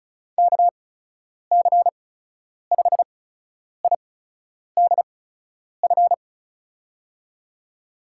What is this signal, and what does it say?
Morse code, heard as a keyed tone, about 36 wpm, spaced as for 10 wpm, 700 Hz. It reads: KC5IDF
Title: USA Call Signs - 6 Characters - 8x Character Spacing - 35wpm